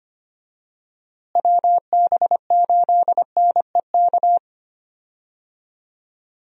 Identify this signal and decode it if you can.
Morse code, heard as a keyed tone, about 25 wpm, 700 Hz. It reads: WB8NEK
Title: USA Call Signs - 6 Characters - Repeated 2x - 25wpm